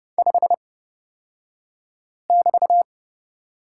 Morse code, heard as a keyed tone, about 29 wpm, spaced as for 5 wpm, 690 Hz. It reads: 5<BT>
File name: Lesson 36d - BT - Two Characters 30wpm